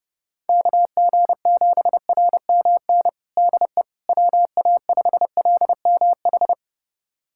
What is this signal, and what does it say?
Morse code, heard as a keyed tone, about 30 wpm, 700 Hz. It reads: KG7RMN DE WA5LMH
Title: USA Callsign DE Callsign - 6 Characters - 30wpm